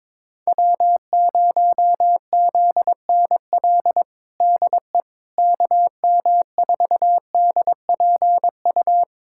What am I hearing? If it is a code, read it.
Morse code, heard as a keyed tone, about 22 wpm, 700 Hz. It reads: W0ZNL DE KM4DPU
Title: USA Callsign DE Callsign 22wpm